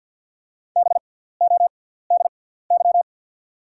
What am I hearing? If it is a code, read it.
Morse code, heard as a keyed tone, about 48 wpm, spaced as for 17 wpm, 690 Hz. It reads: BODY
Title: 6x Character Spacing - Top 500 Words - 50wpm